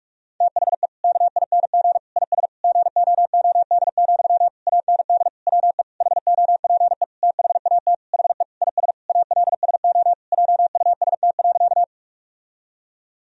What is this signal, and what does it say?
Morse code, heard as a keyed tone, about 42 wpm, 700 Hz. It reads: THE KING IS GOOD, AND WE HOPE THAT HE IS ALSO JUST.
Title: Sentences from Top 1000 Words - Review Part 1 - 45wpm